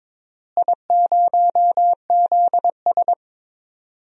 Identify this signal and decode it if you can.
Morse code, heard as a keyed tone, about 22 wpm, 700 Hz. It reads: I0ZS
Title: DXCC-100 Call Signs - 4 Characters - Repeated 3x - 22wpm